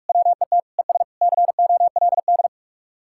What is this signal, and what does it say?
Morse code, heard as a keyed tone, about 48 wpm, 700 Hz. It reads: WET ES COLD